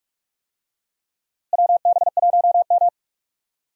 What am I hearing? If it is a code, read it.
Morse code, heard as a keyed tone, about 45 wpm, 700 Hz. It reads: WB1M
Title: USA Call Signs - 4 Characters - Repeated 2x - 45wpm